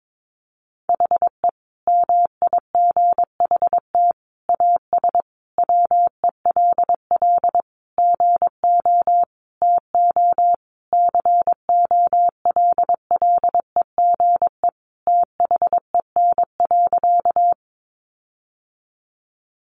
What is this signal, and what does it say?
Morse code, heard as a keyed tone, about 22 wpm, 700 Hz. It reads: HE MIGHT AS WELL GO TO COLLEGE THEN.
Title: Sentences from Top 1300 Words 22wpm